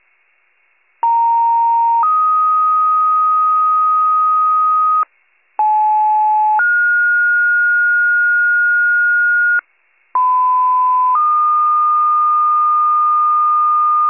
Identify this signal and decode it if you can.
Two-tone page, heard as a one-second tone, then a three-second tone, about 930 then 1250 Hz, about 860 then 1430 Hz, about 1010 then 1220 Hz. It is heard from 1.0 to 14.1 s.